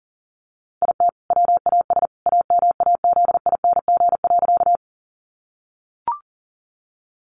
Morse code, heard as a keyed tone, about 37 wpm, 700 Hz. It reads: IT WAS AMAZING.
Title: Sentences from Top 1500 Words 40wpm